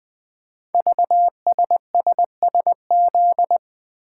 Morse code, heard as a keyed tone, about 20 wpm, 700 Hz. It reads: VSSSZ
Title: Code Groups - 5 Letters - 20wpm